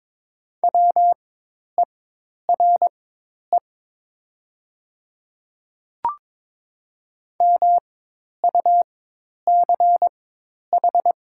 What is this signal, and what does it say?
Morse code, heard as a keyed tone, about 22 wpm, spaced as for 10 wpm, 700 Hz. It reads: WERE MUCH
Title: 4x Character Spacing - Top 200 Words - 22wpm